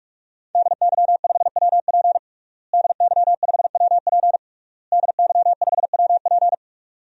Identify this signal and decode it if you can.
Morse code, heard as a keyed tone, about 48 wpm, 700 Hz. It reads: DY5WP DY5WP DY5WP
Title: DXCC-100 Call Signs - 5 Characters - Repeated 3x - 45wpm